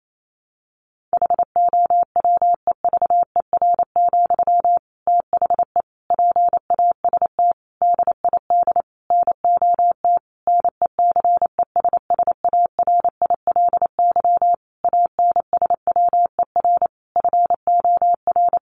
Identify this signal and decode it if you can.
Morse code, heard as a keyed tone, about 28 wpm, 700 Hz. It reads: HOWEVER, THE PAST DID NOT NECESSARILY ANSWER FOR
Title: Sentences from Top 1400 Words 28wpm